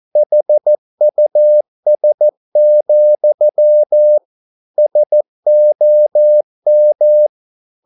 Morse code, heard as a keyed tone, about 14 wpm, 600 Hz. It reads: HUS, SOM
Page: Ferie 13 wpm